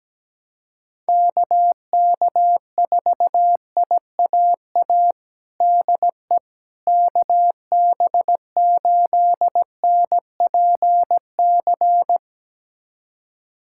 Morse code, heard as a keyed tone, about 17 wpm, 700 Hz. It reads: KK4IAA DE KB8NPC
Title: USA Callsign DE Callsign - 6 Characters - 17wpm